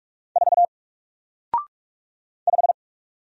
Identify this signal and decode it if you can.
Morse code, heard as a keyed tone, about 46 wpm, 700 Hz. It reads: 4 5
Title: Single Letter-Number Alt 45wpm